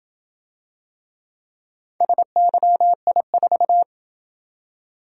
Morse code, heard as a keyed tone, about 27 wpm, 700 Hz. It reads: SYI4